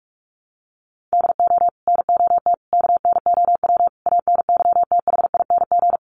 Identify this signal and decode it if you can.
Morse code, heard as a keyed tone, about 44 wpm, 700 Hz. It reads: DO NOT KNOW ANYTHING
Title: Sets of 5 Words 45wpm